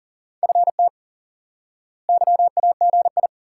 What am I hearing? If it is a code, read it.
Morse code, heard as a keyed tone, about 37 wpm, 700 Hz. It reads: FT YAGI
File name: IQE Course 3 - Lesson 89 - YAGI - Two Elements - 40wpm